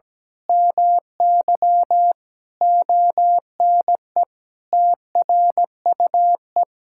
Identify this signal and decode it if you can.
Morse code, heard as a keyed tone, about 17 wpm, 700 Hz. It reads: MY ONE TRUE
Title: Sentences from Top 1000 Words - Review Part 5 - 17wpm